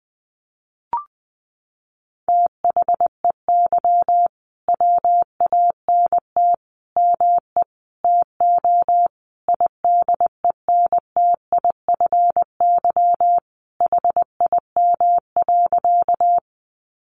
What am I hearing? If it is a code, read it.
Morse code, heard as a keyed tone, about 20 wpm, 700 Hz. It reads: THEY WANT ME TO IDENTIFY HIM.
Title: Sentences from Top 900 Words 20wpm